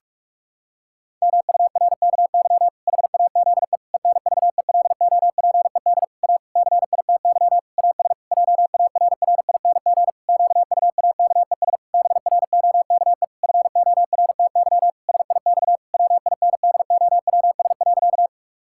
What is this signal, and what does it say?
Morse code, heard as a keyed tone, about 41 wpm, 700 Hz. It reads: MURKY HAZE ENVELOPED A CITY AS JARRING QUAKES BROKE FORTY SIX WINDOWS.
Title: Pangram Challenge v2 - 45wpm